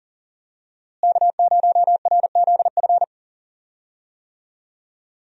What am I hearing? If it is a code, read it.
Morse code, heard as a keyed tone, about 40 wpm, 700 Hz. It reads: K0RZF
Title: USA Call Signs - 5 Characters - Repeated 2x - 40wpm